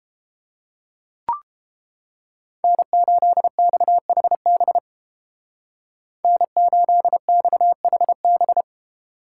Morse code, heard as a keyed tone, about 31 wpm, 700 Hz. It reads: N8XHB N8XHB